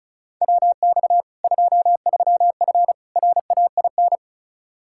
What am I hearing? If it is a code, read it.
Morse code, heard as a keyed tone, about 34 wpm, 700 Hz. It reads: WX 23F RAIN
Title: Practice QSO - Exchange 2 - Info - 35wpm